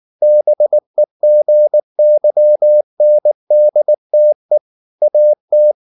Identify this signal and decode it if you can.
Morse code, heard as a keyed tone, about 19 wpm, 600 Hz. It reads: BEGYNDTE AT
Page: Ferie 19 wpm